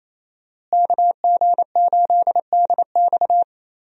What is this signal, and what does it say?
Morse code, heard as a keyed tone, about 27 wpm, 700 Hz. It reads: KG8DX
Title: USA Call Signs - 5 Characters - Repeated 2x - 28wpm